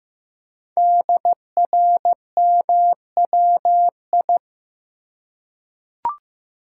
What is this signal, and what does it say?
Morse code, heard as a keyed tone, about 15 wpm, 700 Hz. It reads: DRMWI
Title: Code Groups - 5 Letters - 15wpm